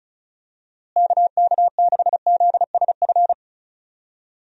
Morse code, heard as a keyed tone, about 36 wpm, 700 Hz. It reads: KK6ZSF
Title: USA Call Signs - 6 Characters - Repeated 2x - 35wpm